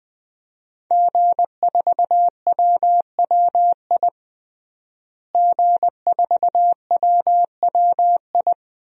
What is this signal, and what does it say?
Morse code, heard as a keyed tone, about 20 wpm, 700 Hz. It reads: G4WWI G4WWI
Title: DXCC-100 Call Signs - 5 Characters - Repeated 3x - 20wpm